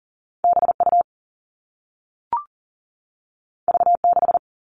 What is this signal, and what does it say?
Morse code, heard as a keyed tone, about 40 wpm, 700 Hz. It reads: BU V6
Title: All Cut Numbers - 2-Digit 40wpm